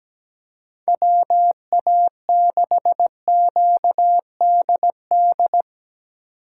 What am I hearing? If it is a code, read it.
Morse code, heard as a keyed tone, about 17 wpm, 700 Hz. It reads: WA6QDD
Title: USA Call Signs - 6 Characters - 2x Character Spacing - 17wpm